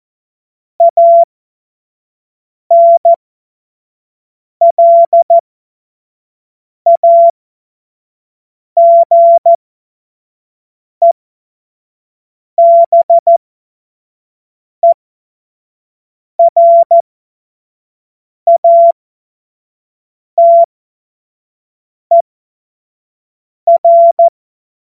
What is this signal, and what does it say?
Morse code, heard as a keyed tone, about 14 wpm, spaced as for 5 wpm, 680 Hz. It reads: ANLAGEBERATER